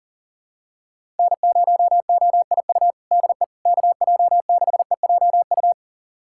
Morse code, heard as a keyed tone, about 40 wpm, 700 Hz. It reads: N0OIU DE KJ6EJU
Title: USA Callsign DE Callsign 40wpm